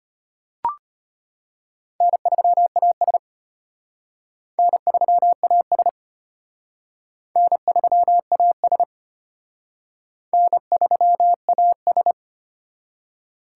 Morse code, heard as a keyed tone, about 31 wpm, 700 Hz. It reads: N3AS N3AS N3AS N3AS